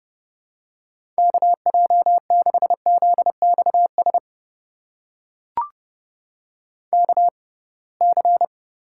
Morse code, heard as a keyed tone, about 30 wpm, spaced as for 23 wpm, 700 Hz. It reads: KJ6ZXS KC
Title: USA Call Signs - 6 Characters - 6x Character Spacing - 30wpm